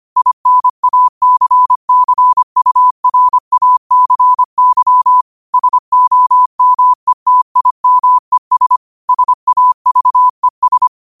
Morse code, heard as a keyed tone, about 25 wpm, 1000 Hz. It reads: INACCURACY SOMETIMES SAVES
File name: qotdfiles / 25wpm